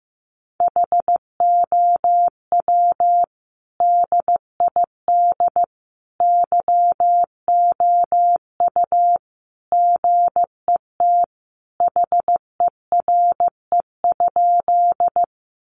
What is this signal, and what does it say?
Morse code, heard as a keyed tone, about 15 wpm, 700 Hz. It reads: HOW DID YOU GET HERE?
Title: Sentences from Top 200 Words 15wpm